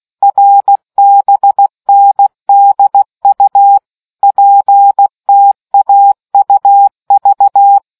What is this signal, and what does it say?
Morse code, heard as a keyed tone, about 16 wpm, 800 Hz. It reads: RBNDU PTAUV